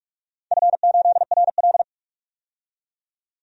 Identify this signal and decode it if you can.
Morse code, heard as a keyed tone, about 44 wpm, 700 Hz. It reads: F8RL
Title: DXCC-100 Call Signs - 4 Characters - 45wpm